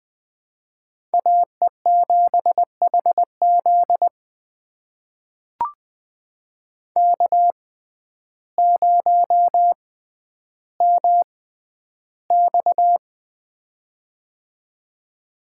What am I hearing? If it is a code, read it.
Morse code, heard as a keyed tone, about 21 wpm, spaced as for 11 wpm, 700 Hz. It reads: AE7HZ K0MX
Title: USA Call Signs - 6x Character Spacing - 20wpm